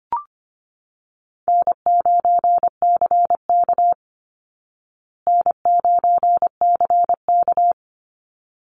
Morse code, heard as a keyed tone, about 25 wpm, 700 Hz. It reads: N9CK N9CK
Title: Easy Call Signs Repeated 2x - 25wpm